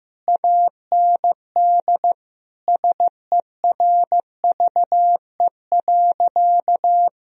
Morse code, heard as a keyed tone, about 15 wpm, 700 Hz. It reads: AND SERVE.
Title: Sentences from Top 1000 Words - Review Part 2 - 15wpm